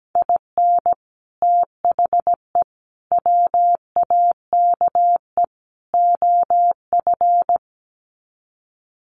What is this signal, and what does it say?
Morse code, heard as a keyed tone, about 17 wpm, 700 Hz. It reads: IN THE WAKE OF
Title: Sets of 4 Words 17wpm